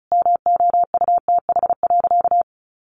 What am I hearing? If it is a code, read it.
Morse code, heard as a keyed tone, about 34 wpm, 700 Hz. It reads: MOUTH.
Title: Sentences from Top 1500 Words 35wpm